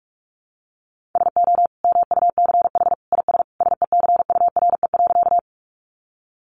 Morse code, heard as a keyed tone, about 44 wpm, 700 Hz. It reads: SO MUCH IS SECURE.
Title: Sentences from Top 1200 Words 45wpm